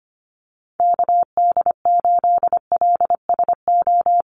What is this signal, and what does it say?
Morse code, heard as a keyed tone, about 25 wpm, 700 Hz. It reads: KD8LSO
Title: USA Call Signs Repeated 2x 25wpm